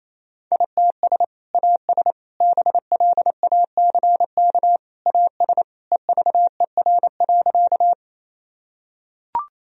Morse code, heard as a keyed tone, about 28 wpm, 690 Hz. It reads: ITS AS BLACK AS EVER.
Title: Sentences from Top 1000 Words - Review Part 2 - 28wpm